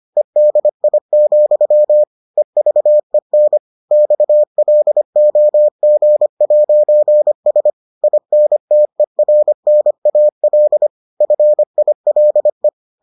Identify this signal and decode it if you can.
Morse code, heard as a keyed tone, about 25 wpm, 600 Hz. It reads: EDI, EVEN XLOG'S INTERNAL FILE